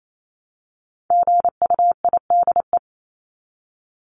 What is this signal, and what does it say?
Morse code, heard as a keyed tone, about 28 wpm, 700 Hz. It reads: GUIDE
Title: Top 501-750 Words 28wpm